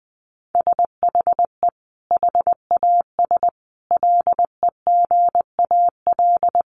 Morse code, heard as a keyed tone, about 20 wpm, 700 Hz. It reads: SHE HAS LEGAL
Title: Sentences from Top 1200 Words 20wpm